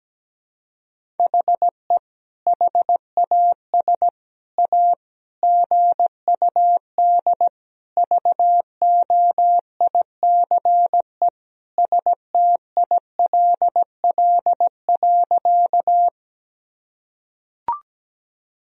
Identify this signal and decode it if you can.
Morse code, heard as a keyed tone, about 17 wpm, 700 Hz. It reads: HE HAS A GUD VOICE STILL.
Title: Sentences from Top 500 Words and Abbreviations 17wpm